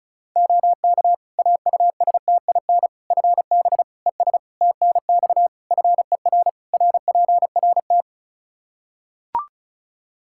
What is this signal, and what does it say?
Morse code, heard as a keyed tone, about 36 wpm, 700 Hz. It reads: OK AUSTIN FB ES TNX FER RPRT
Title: Practice QSO - Exchange 2 - Info - 35wpm